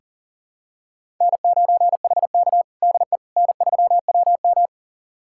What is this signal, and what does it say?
Morse code, heard as a keyed tone, about 40 wpm, 700 Hz. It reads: N9HK DE N3WM
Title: USA Callsign DE Callsign - 4 Characters - 40wpm